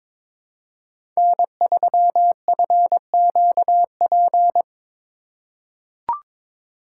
Morse code, heard as a keyed tone, about 21 wpm, 700 Hz. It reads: N3FQP
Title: USA Call Signs - 5 Characters - 4x Character Spacing - 22wpm